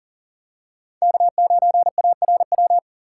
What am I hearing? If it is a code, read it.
Morse code, heard as a keyed tone, about 38 wpm, 700 Hz. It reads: K9ARW
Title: USA Call Signs - 5 Characters - 3x Character Spacing - 40wpm